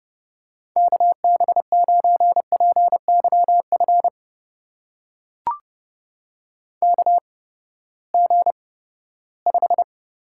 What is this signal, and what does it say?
Morse code, heard as a keyed tone, about 31 wpm, spaced as for 17 wpm, 700 Hz. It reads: KB9PYF KG5
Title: USA Call Signs - 6 Characters - 8x Character Spacing - 30wpm